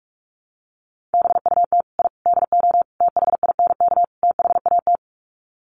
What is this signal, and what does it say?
Morse code, heard as a keyed tone, about 44 wpm, 700 Hz. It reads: BUT I DO THINK THAT
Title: Sets of 5 Words Encore2 45wpm